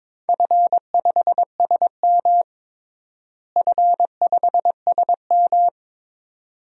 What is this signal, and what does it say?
Morse code, heard as a keyed tone, about 22 wpm, 700 Hz. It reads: F5SM F5SM
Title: DXCC-100 Call Signs - 4 Characters - Repeated 2x - 22wpm